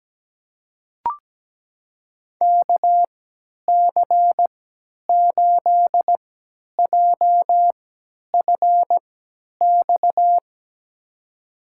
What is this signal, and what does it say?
Morse code, heard as a keyed tone, about 17 wpm, spaced as for 10 wpm, 700 Hz. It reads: KC8JFX
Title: USA Call Signs - 6 Characters - 3x Character Spacing - 17wpm